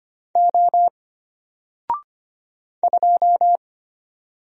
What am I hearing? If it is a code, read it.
Morse code, heard as a keyed tone, about 25 wpm, 700 Hz. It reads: O 2
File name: Single-Letter-Number-25wpm